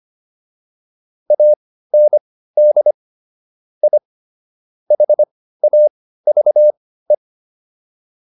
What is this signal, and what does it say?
Morse code, heard as a keyed tone, about 25 wpm, spaced as for 15 wpm, 600 Hz. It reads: AND I HAVE
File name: WD204 15